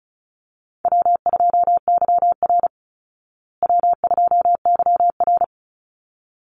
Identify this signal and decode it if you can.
Morse code, heard as a keyed tone, about 34 wpm, 700 Hz. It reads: W2YR W2YR
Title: Call Signs Repeated 3x - 35wpm